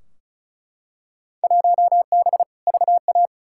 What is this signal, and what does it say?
Morse code, heard as a keyed tone, about 36 wpm, 700 Hz. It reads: 1B VA
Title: Field Day Exchanges - 35wpm